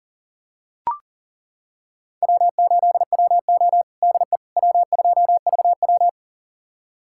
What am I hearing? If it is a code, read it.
Morse code, heard as a keyed tone, about 40 wpm, 700 Hz. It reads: W8WO DE W2VW
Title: USA Callsign DE Callsign - 4 Characters - 40wpm